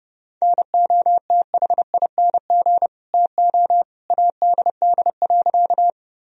Morse code, heard as a keyed tone, about 30 wpm, 700 Hz. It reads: NOTHING TO ADD.
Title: Sentences from Top 1000 Words - Review Part 1 - 30wpm